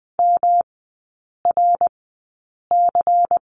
Morse code, heard as a keyed tone, about 20 wpm, spaced as for 8 wpm, 700 Hz. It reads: MRC
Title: 107 Lesson 18e - Letter m - Three Characters 20wpm